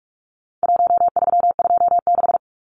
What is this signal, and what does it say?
Morse code, heard as a keyed tone, about 44 wpm, 700 Hz. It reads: 1326
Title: Numbers - 4-Digit 45wpm